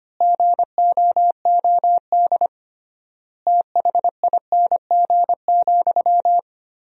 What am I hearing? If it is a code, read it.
Morse code, heard as a keyed tone, about 25 wpm, 700 Hz. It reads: GOOD THING,
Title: 3x Word-Spacing - Sentences from Top 200 Words - 25wpm